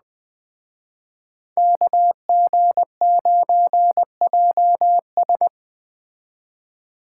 Morse code, heard as a keyed tone, about 21 wpm, 700 Hz. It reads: KG9JS
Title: Speed Racing - USA Call Signs 17wpm